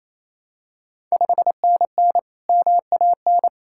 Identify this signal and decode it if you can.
Morse code, heard as a keyed tone, about 27 wpm, 690 Hz. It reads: 5NN MAN
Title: Ontario QSO Party Exchange 28wpm